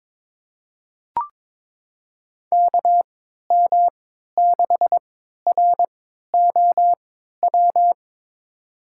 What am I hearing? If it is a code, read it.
Morse code, heard as a keyed tone, about 22 wpm, spaced as for 13 wpm, 700 Hz. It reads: KM6ROW
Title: USA Call Signs - 3x Character Spacing - 22wpm